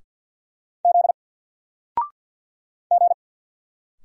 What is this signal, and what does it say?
Morse code, heard as a keyed tone, about 48 wpm, 700 Hz. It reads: Z G
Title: Single Letter-Number Alt 50wpm